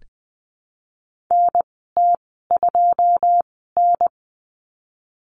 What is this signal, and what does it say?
Morse code, heard as a keyed tone, about 20 wpm, spaced as for 15 wpm, 700 Hz. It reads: NT2N
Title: 082 Lesson 14g - Number 2 - Call Signs 20wpm